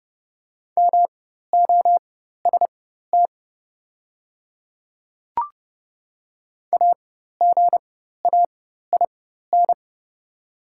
Morse code, heard as a keyed tone, about 30 wpm, spaced as for 14 wpm, 700 Hz. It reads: MOST AGAIN
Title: 4x Character Spacing - Top 300 Words - 30wpm